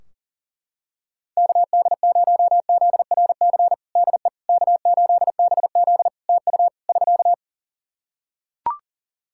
Morse code, heard as a keyed tone, about 37 wpm, 700 Hz. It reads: KD0ZRC DE K8BZ TU <SK>
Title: Practice QSO - Exchange 4 - Ending - 40wpm